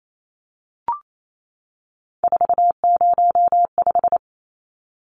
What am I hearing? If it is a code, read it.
Morse code, heard as a keyed tone, about 28 wpm, 700 Hz. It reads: 405
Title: Numbers - 3-Digit 28wpm